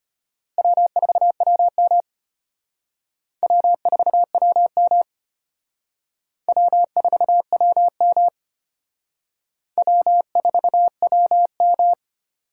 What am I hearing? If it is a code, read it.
Morse code, heard as a keyed tone, about 31 wpm, 700 Hz. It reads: W4WM W4WM W4WM W4WM
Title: Speed Racing - USA 1x2 Call Signs 25wpm